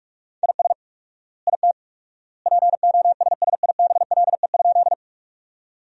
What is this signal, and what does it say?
Morse code, heard as a keyed tone, about 41 wpm, 690 Hz. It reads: IS IT POSSIBLE?
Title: 4x Word-Spacing - Sentences from Top 500 Words - 45wpm